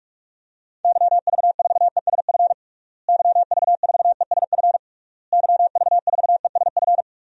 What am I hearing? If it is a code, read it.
Morse code, heard as a keyed tone, about 46 wpm, 700 Hz. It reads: YV4ESF YV4ESF YV4ESF
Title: DXCC-100 Call Signs - 6 Characters - Repeated 3x - 45wpm